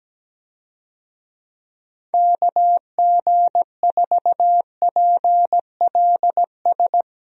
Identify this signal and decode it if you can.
Morse code, heard as a keyed tone, about 17 wpm, 700 Hz. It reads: KG4PLS
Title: USA Call Signs - 6 Characters - Repeated 3x - 17wpm